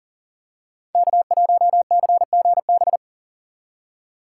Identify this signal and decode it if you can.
Morse code, heard as a keyed tone, about 42 wpm, 700 Hz. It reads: K1CGB
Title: Code Groups - 5 Letter:Numbers - 40wpm